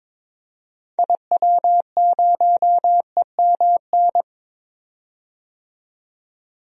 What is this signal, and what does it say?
Morse code, heard as a keyed tone, about 21 wpm, 700 Hz. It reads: IW0EMN
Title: DXCC-100 Call Signs - 6 Characters - 22wpm